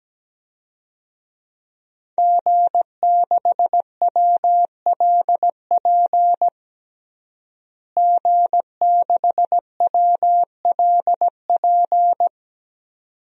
Morse code, heard as a keyed tone, about 17 wpm, 700 Hz. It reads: G6WLP G6WLP
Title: DXCC-100 Call Signs - 5 Characters - Repeated 2x - 17wpm